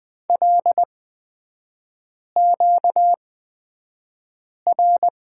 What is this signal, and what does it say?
Morse code, heard as a keyed tone, about 20 wpm, spaced as for 5 wpm, 700 Hz. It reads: LQR
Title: 236 Lesson 38e - Letter q - Three Characters 20wpm